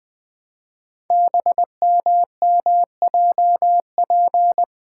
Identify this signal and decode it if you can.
Morse code, heard as a keyed tone, about 20 wpm, 700 Hz. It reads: BMMJP
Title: Code Groups - 5 Letter:Numbers - 20wpm